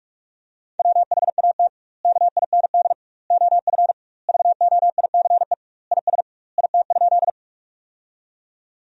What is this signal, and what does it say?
Morse code, heard as a keyed tone, about 46 wpm, 700 Hz. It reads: WHAT KIND OF VOICE IS IT?
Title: 2x Word-Spacing - Sentences from Top 500 Words - 45wpm